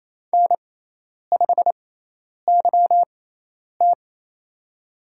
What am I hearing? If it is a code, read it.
Morse code, heard as a keyed tone, about 28 wpm, spaced as for 10 wpm, 700 Hz. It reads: N5YT
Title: USA Call Signs - 4 Characters - 6x Character Spacing - 28wpm